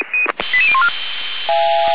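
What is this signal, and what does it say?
Five-tone selective call (ZVEI), heard as quick runs of five tones, tones of about 70 ms. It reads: 88014
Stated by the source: ZVEI five-tone call received by SDR, from server of Feuerwehr Orlamünde (fire brigade).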